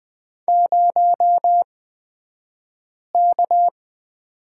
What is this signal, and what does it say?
Morse code, heard as a keyed tone, about 20 wpm, spaced as for 5 wpm, 700 Hz. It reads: 0K
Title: 235 Lesson 38d - Letter q - Two Characters 20wpm